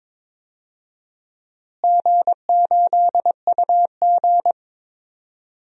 Morse code, heard as a keyed tone, about 21 wpm, 700 Hz. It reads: G8UG
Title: DXCC-100 Call Signs - 4 Characters - Repeated 2x - 22wpm